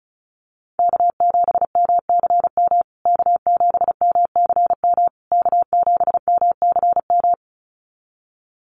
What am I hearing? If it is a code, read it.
Morse code, heard as a keyed tone, about 34 wpm, 700 Hz. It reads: K7MCM K7MCM K7MCM
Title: USA Call Signs Repeated 3x 35wpm